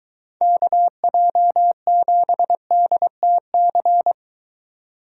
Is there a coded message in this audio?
Yes: Morse code from 0.4 to 4.1 s, heard as a keyed tone, about 23 wpm, 700 Hz. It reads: KJ7DTC